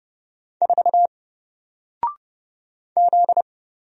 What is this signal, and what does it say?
Morse code, heard as a keyed tone, about 30 wpm, 700 Hz. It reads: 4 Z
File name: Single-Letter-Number-30wpm